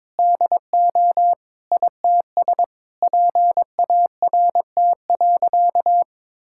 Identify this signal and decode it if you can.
Morse code, heard as a keyed tone, about 22 wpm, 700 Hz. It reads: DO ITS PART.
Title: Sentences from Top 1000 Words - Review Part 5 - 22wpm